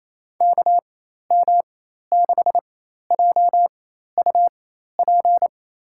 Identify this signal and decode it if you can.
Morse code, heard as a keyed tone, about 28 wpm, spaced as for 13 wpm, 700 Hz. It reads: KM6JUP
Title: USA Call Signs - 6 Characters - 4x Character Spacing - 28wpm